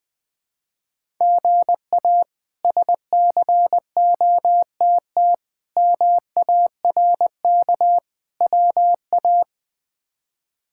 Morse code, heard as a keyed tone, about 20 wpm, 700 Hz. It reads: GA SCOTT MARK WA
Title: K1USN SST QSO Practice - 20wpm